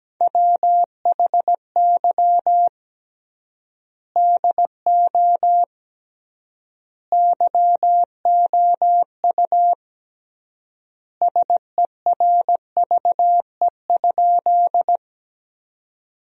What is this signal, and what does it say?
Morse code, heard as a keyed tone, about 17 wpm, 700 Hz. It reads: WHY DO YOU SERVE?
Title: 3x Word-Spacing - Sentences from Top 400 Words - 17wpm